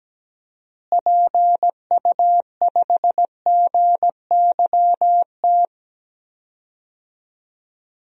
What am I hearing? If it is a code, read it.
Morse code, heard as a keyed tone, about 17 wpm, 700 Hz. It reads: PU5GYT
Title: DXCC-100 Call Signs - 6 Characters - Repeated 2x - 17wpm